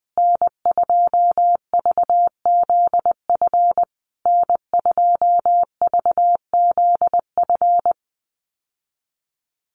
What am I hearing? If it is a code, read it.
Morse code, heard as a keyed tone, about 20 wpm, 700 Hz. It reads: N2VZF N2VZF
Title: USA Call Signs Repeated 3x 20wpm